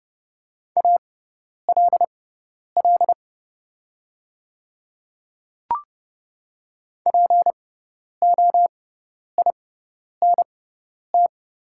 Morse code, heard as a keyed tone, about 30 wpm, spaced as for 10 wpm, 690 Hz. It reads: ALL POINT
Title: 6x Character Spacing - Top 300 Words - 30wpm